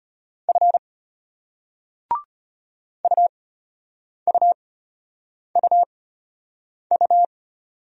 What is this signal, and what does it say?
Morse code, heard as a keyed tone, about 34 wpm, spaced as for 8 wpm, 700 Hz. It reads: F UUUU